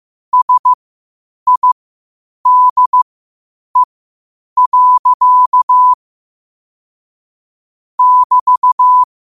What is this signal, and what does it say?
Morse code, heard as a keyed tone, about 15 wpm, spaced as for 8 wpm, 1000 Hz. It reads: SIDE. <BT>
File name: qotdfiles / 8wpm